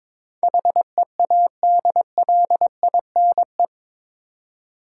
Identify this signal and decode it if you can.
Morse code, heard as a keyed tone, about 21 wpm, 700 Hz. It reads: HEADLINE
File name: practice / 22wpm / Eight Letter Words - 22wpm